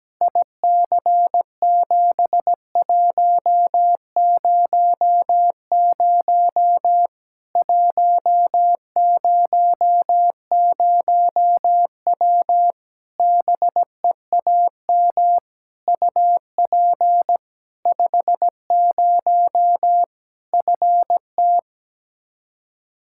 Morse code, heard as a keyed tone, about 17 wpm, 700 Hz. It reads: IC7100 100W BEAM UP 50 FT
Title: Practice QSO - Exchange 2 - Info - 17wpm